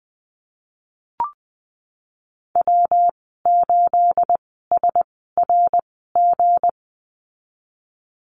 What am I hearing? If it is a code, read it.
Morse code, heard as a keyed tone, about 20 wpm, spaced as for 15 wpm, 700 Hz. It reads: W8SRG
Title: 213 Lesson 34g - Number 8 - Call Signs 20wpm